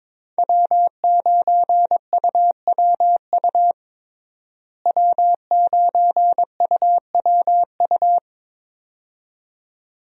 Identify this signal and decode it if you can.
Morse code, heard as a keyed tone, about 22 wpm, 700 Hz. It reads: W9UWU W9UWU
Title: USA Call Signs - 5 Characters - Repeated 2x - 22wpm